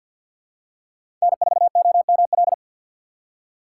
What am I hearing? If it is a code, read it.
Morse code, heard as a keyed tone, about 53 wpm, 700 Hz. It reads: N4OML
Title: USA Call Signs - 5 Characters - 4x Character Spacing - 50wpm